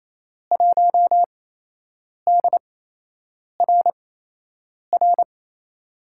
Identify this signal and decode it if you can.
Morse code, heard as a keyed tone, about 28 wpm, spaced as for 8 wpm, 700 Hz. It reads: 1DRR
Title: USA Call Signs - 5 Characters - 8x Character Spacing - 28wpm